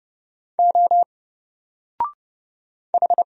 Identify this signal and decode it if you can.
Morse code, heard as a keyed tone, about 30 wpm, 700 Hz. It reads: O H
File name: Single-Letter-Number-30wpm